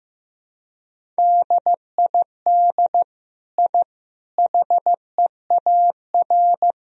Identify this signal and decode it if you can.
Morse code, heard as a keyed tone, about 16 wpm, 700 Hz. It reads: DID I HEAR
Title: Sentences from Top 1000 Words - Review Part 2 - 15wpm